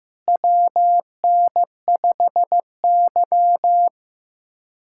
Morse code, heard as a keyed tone, about 15 wpm, 700 Hz. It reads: WN5Y
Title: USA Callsign DE Callsign - 4 Characters - 15wpm